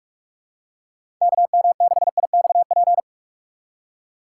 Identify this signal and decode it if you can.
Morse code, heard as a keyed tone, about 47 wpm, 700 Hz. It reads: KM6IXP
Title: USA Call Signs - 6 Characters - 3x Character Spacing - 45wpm